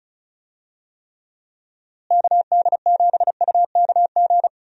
Morse code, heard as a keyed tone, about 36 wpm, 700 Hz. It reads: KD7UKG
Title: USA Call Signs - 6 Characters - Repeated 2x - 35wpm